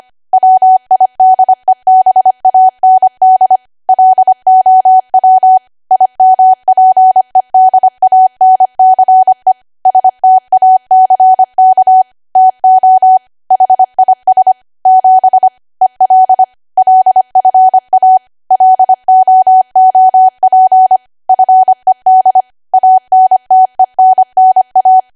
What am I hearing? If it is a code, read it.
Morse code, heard as a keyed tone, about 25 wpm, 750 Hz. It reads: WIDEBAND LOW IMPEDANCE STACK TO HIS 7 EL LFA LOOP FED ANTENNA
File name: cwa / W1AW-Level3 / 25WPM